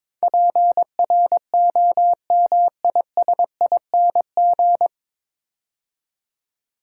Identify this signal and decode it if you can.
Morse code, heard as a keyed tone, about 22 wpm, 700 Hz. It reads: PROMISING
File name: 9 Letter Words - First 3 Letters Given - 22wpm